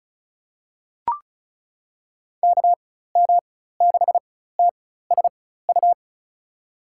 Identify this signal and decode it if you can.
Morse code, heard as a keyed tone, about 34 wpm, spaced as for 16 wpm, 700 Hz. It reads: KM6TSU
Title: USA Call Signs - 6 Characters - 4x Character Spacing - 35wpm